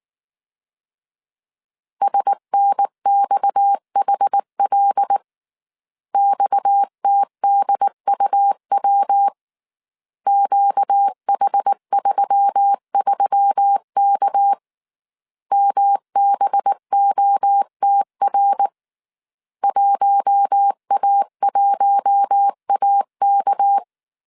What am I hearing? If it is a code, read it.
Morse code, heard as a keyed tone, about 19 wpm, 800 Hz. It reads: SNXHL XTDUW QH33K MBOTR 1A1AK
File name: CW / DET / 1 6 1500C S C 18wpm 2356